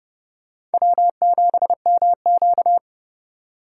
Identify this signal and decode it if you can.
Morse code, heard as a keyed tone, about 31 wpm, 700 Hz. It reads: W7MQ